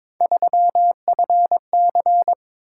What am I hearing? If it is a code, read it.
Morse code, heard as a keyed tone, about 21 wpm, 700 Hz. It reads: 3FC
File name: practice / 22wpm / Three Letter-Number Combinations - 22wpm